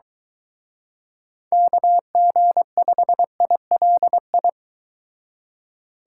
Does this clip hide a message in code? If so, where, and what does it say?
Yes: Morse code from 1.5 to 4.5 s, heard as a keyed tone, about 23 wpm, 700 Hz. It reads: KG5ILI